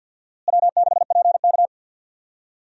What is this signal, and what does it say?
Morse code, heard as a keyed tone, about 52 wpm, 700 Hz. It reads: W6PK
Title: USA Call Signs - 4 Characters - Repeated 2x - 50wpm